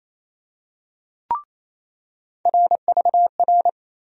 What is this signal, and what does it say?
Morse code, heard as a keyed tone, about 28 wpm, 700 Hz. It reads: RVR
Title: USA Call Sign Postfixes - 3 Letters 28wpm